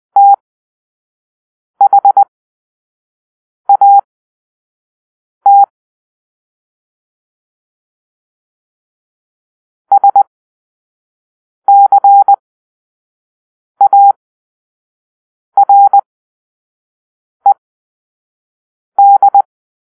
Morse code, heard as a keyed tone, about 20 wpm, spaced as for 5 wpm, 800 Hz. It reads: THAT SCARED
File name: arrl / Grp-2 Drill-4